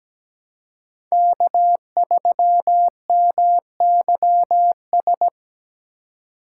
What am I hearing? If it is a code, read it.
Morse code, heard as a keyed tone, about 17 wpm, 700 Hz. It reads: K3MYS
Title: USA Call Signs - 5 Characters - 4x Character Spacing - 17wpm